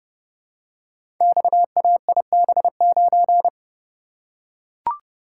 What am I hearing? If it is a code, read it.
Morse code, heard as a keyed tone, about 30 wpm, 700 Hz. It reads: XAIB9
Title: Code Groups - 5 Letter:Numbers - 30wpm